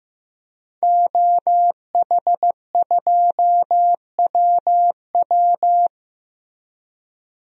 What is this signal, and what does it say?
Morse code, heard as a keyed tone, about 16 wpm, 700 Hz. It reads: OH2WW
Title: DXCC-100 Call Signs - 5 Characters - Repeated 3x - 15wpm